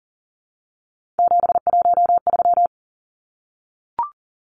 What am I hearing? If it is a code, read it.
Morse code, heard as a keyed tone, about 40 wpm, 700 Hz. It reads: 713
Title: Numbers - 3-Digit 40wpm